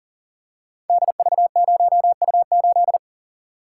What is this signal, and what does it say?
Morse code, heard as a keyed tone, about 40 wpm, 700 Hz. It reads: DV0U8
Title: Code Groups - 5 Letter:Numbers - 40wpm